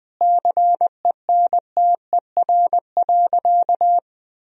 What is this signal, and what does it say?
Morse code, heard as a keyed tone, about 20 wpm, 700 Hz. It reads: CENTER.
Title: Sentences from Top 1000 Words - Review Part 2 - 20wpm